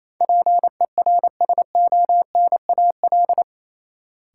Morse code, heard as a keyed tone, about 28 wpm, 700 Hz. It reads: PERSONAL
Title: Sentences from Top 700 Words - New Word Drill - 28wpm